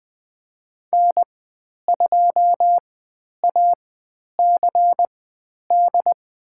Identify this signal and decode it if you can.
Morse code, heard as a keyed tone, about 20 wpm, spaced as for 10 wpm, 700 Hz. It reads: N2ACD
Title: 095 Lesson 16g - Letter c - Call Signs 20wpm